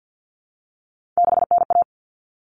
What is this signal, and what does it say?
Morse code, heard as a keyed tone, about 50 wpm, 700 Hz. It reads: 6NA
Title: Common Cut Numbers - 3-Digit 50wpm